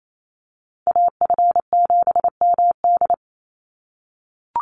Morse code, heard as a keyed tone, about 28 wpm, 700 Hz. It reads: AF7MD
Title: NA SOTA Chaser Callsigns 28wpm